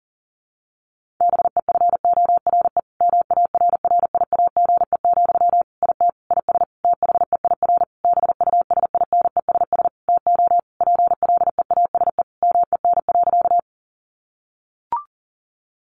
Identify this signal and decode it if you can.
Morse code, heard as a keyed tone, about 40 wpm, 700 Hz. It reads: BEFORE MARRIAGE, IT IS THEIR BUSINESS TO PLEASE MEN.
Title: Sentences from Top 1500 Words 40wpm